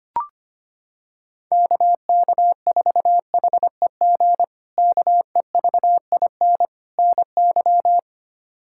Morse code, heard as a keyed tone, about 25 wpm, 700 Hz. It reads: KK4HEG KEVIN NY
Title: K1USN SST QSO Practice - 25wpm